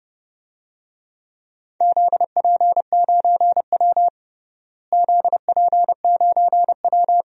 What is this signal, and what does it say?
Morse code, heard as a keyed tone, about 31 wpm, 700 Hz. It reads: ZP9W ZP9W
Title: DXCC-100 Call Signs - 4 Characters - Repeated 3x - 30wpm